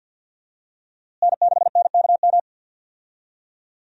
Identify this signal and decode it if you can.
Morse code, heard as a keyed tone, about 51 wpm, 700 Hz. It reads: N6NKM
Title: USA Call Signs - 5 Characters - Repeated 2x - 50wpm